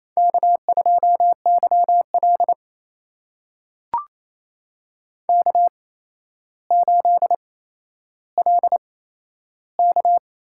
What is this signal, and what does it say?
Morse code, heard as a keyed tone, about 29 wpm, spaced as for 12 wpm, 700 Hz. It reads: K2YL K8LK
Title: USA Call Signs - 4 Characters - 8x Character Spacing - 28wpm